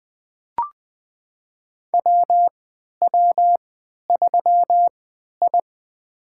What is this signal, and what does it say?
Morse code, heard as a keyed tone, about 20 wpm, spaced as for 11 wpm, 700 Hz. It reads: WW3I
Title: USA Call Signs - 3x Character Spacing - 20wpm